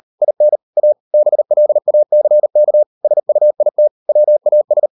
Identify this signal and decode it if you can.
Morse code, heard as a keyed tone, about 38 wpm, 600 Hz. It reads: IN A BLACK SUIT WAS